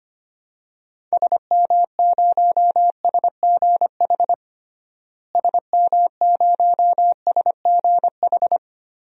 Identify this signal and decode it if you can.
Morse code, heard as a keyed tone, about 25 wpm, 700 Hz. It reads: SM0SGH SM0SGH
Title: DXCC-100 Call Signs - 6 Characters - Repeated 2x - 25wpm